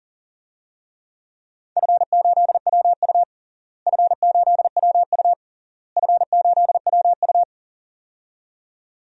Morse code, heard as a keyed tone, about 37 wpm, 700 Hz. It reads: F8WU F8WU F8WU
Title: DXCC-100 Call Signs - 4 Characters - Repeated 3x - 40wpm